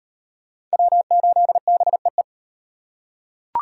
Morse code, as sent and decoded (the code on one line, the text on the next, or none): .-- ---.. -... .
W8BEE